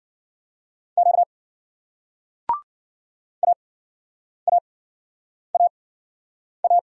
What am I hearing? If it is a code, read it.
Morse code, heard as a keyed tone, about 50 wpm, spaced as for 9 wpm, 700 Hz. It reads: Q AAAA